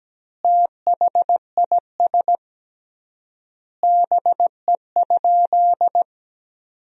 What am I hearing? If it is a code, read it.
Morse code, heard as a keyed tone, about 17 wpm, 700 Hz. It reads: THIS BE?
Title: 3x Word-Spacing - Sentences from Top 100 Words - 17wpm